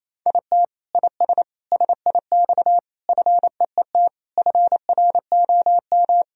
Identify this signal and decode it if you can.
Morse code, heard as a keyed tone, about 27 wpm, 700 Hz. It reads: IT IS SIX FEET FROM
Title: Sentences from Top 1000 Words - Review Part 2 - 28wpm